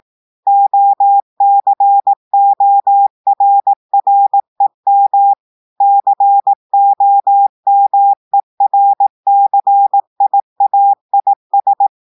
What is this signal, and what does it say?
Morse code, heard as a keyed tone, about 18 wpm, 800 Hz. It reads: OCORREM COMERCIAIS
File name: audios / portugais / portugais texte en clair 2 vitesse 18 mots minutes